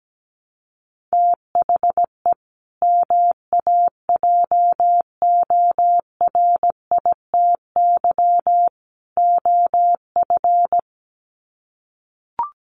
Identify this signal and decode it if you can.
Morse code, heard as a keyed tone, about 17 wpm, 700 Hz. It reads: THE MAJORITY OF
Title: Sets of 3 Words 17wpm